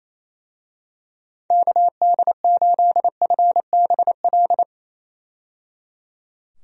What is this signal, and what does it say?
Morse code, heard as a keyed tone, about 28 wpm, 700 Hz. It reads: KD8FBL
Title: USA Call Signs - 6 Characters - 28wpm